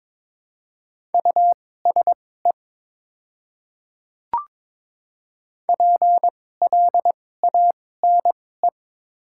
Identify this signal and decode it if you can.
Morse code, heard as a keyed tone, about 22 wpm, spaced as for 16 wpm, 700 Hz. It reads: USE PLANE
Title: 2x Character Spacing - Top 500 Words - 22wpm